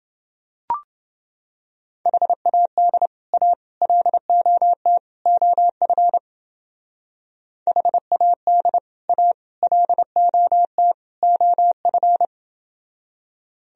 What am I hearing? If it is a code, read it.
Morse code, heard as a keyed tone, about 30 wpm, 700 Hz. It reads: HAD A LOT OF HAD A LOT OF